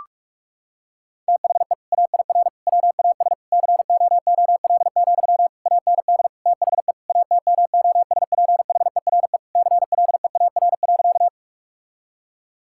Morse code, heard as a keyed tone, about 41 wpm, 700 Hz. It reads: THE AIR WAS COOL, AND THE ATMOSPHERE CLEAR.
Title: Sentences from Top 1600 Words 45wpm